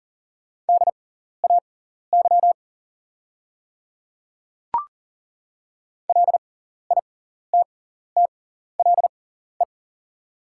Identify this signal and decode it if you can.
Morse code, heard as a keyed tone, about 40 wpm, spaced as for 14 wpm, 700 Hz. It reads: DAY LITTLE
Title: 6x Character Spacing - Top 300 Words - 40wpm